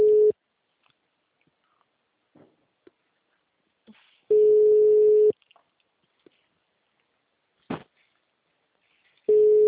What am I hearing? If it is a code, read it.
Ringback tone (Europe, 425 Hz).
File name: out- -2029- - - .71555